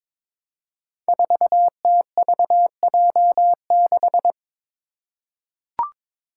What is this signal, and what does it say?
Morse code, heard as a keyed tone, about 22 wpm, 700 Hz. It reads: 4TVJ6
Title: Five Challenging Letter-Number Combinations 22wpm